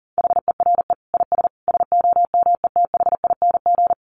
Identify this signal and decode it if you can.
Morse code, heard as a keyed tone, about 40 wpm, 700 Hz. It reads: HERE IS SOMETHING
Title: Sentences from Top 1000 Words 40wpm